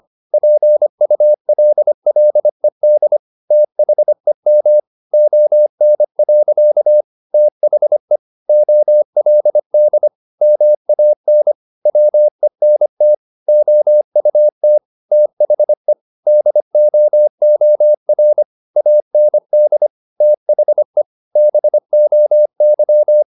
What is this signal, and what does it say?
Morse code, heard as a keyed tone, about 25 wpm, 600 Hz. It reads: PULLED THEM ON. THE OLD MAN WENT OUT THE DOOR AND THE BOY
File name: the old man and the sea part 1 25wpm